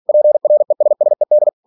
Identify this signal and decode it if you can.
Morse code, heard as a keyed tone, about 46 wpm, 600 Hz. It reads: PRESSED
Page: Never 47wpm